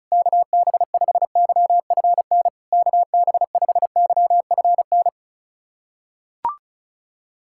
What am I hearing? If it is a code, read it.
Morse code, heard as a keyed tone, about 36 wpm, 700 Hz. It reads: KB5YFN KB5YFN
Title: SOTA QSO Practice 35wpm